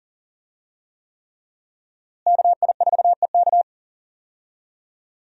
Morse code, heard as a keyed tone, about 40 wpm, 700 Hz. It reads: KI4EK
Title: USA Call Signs - 5 Characters - Repeated 3x - 40wpm